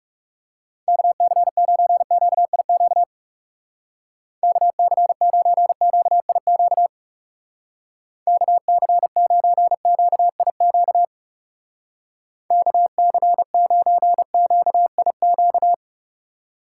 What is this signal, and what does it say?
Morse code, heard as a keyed tone, about 38 wpm, 700 Hz. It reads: KC9QIQ KC9QIQ KC9QIQ KC9QIQ